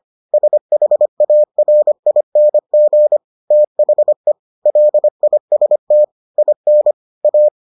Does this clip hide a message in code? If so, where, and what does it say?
Yes: Morse code from 0.3 to 7.5 s, heard as a keyed tone, about 25 wpm, 600 Hz. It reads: SHARING THE LIST IN A